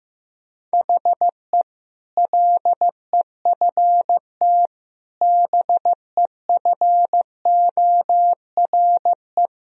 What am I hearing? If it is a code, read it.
Morse code, heard as a keyed tone, about 15 wpm, 700 Hz. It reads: HE LEFT BEFORE
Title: Sentences from Top 1000 Words - Review Part 4 - 15wpm